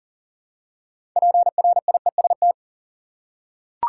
Morse code, heard as a keyed tone, about 37 wpm, 700 Hz. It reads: PRIEST
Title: Six Letter Words - 40wpm